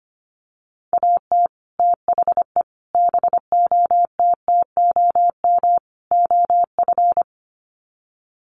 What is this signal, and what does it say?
Morse code, heard as a keyed tone, about 25 wpm, 700 Hz. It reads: AT THE BOTTOM OF
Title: Sets of 4 Words 25wpm